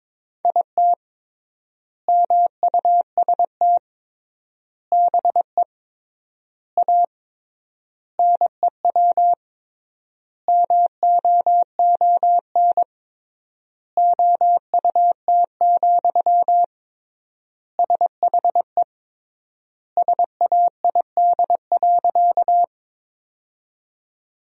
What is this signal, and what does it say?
Morse code, heard as a keyed tone, about 22 wpm, 700 Hz. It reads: IT MUST BE A NEW MOON OUT, SHE SAID.
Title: 3x Word-Spacing - Sentences from Top 500 Words - 22wpm